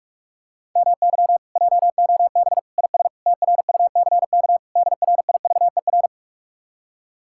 Morse code, heard as a keyed tone, about 45 wpm, 700 Hz. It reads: MY JOB IS TRUCK DRIVER